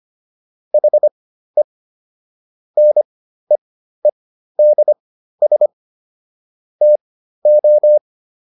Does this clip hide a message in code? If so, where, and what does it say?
Yes: Morse code from 0.7 to 8.0 s, heard as a keyed tone, about 25 wpm, spaced as for 13 wpm, 600 Hz. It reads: HE NEEDS TO